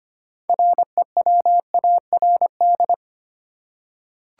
Morse code, heard as a keyed tone, about 25 wpm, 700 Hz. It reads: REWARD
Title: Six Letter Words - 25wpm